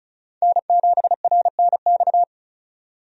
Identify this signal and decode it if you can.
Morse code, heard as a keyed tone, about 36 wpm, 700 Hz. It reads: N7RNX
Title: USA Call Signs - 5 Characters - 35wpm